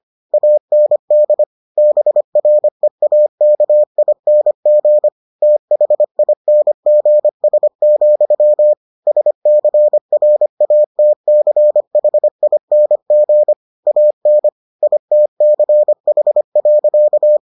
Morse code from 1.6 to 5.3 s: BREAKING

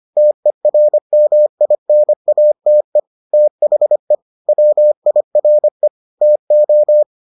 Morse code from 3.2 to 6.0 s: THE WIRE

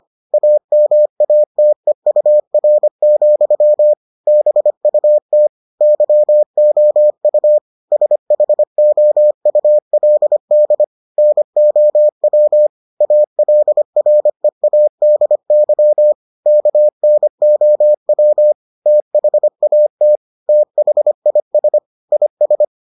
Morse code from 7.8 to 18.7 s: SHOULD NOW ALREADY KNOW

